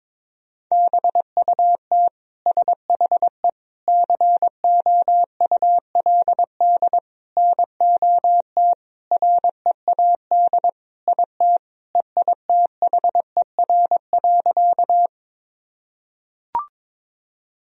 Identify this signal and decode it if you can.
Morse code, heard as a keyed tone, about 22 wpm, 700 Hz. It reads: BUT SHE COULD NOT READ IT EITHER.
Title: Sentences from Top 1000 Words - Review Part 4 - 22wpm